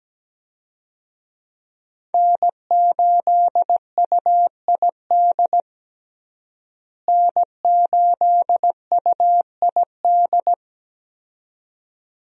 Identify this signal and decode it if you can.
Morse code, heard as a keyed tone, about 17 wpm, 700 Hz. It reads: N8UID N8UID
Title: USA Call Signs - 5 Characters - Repeated 2x - 17wpm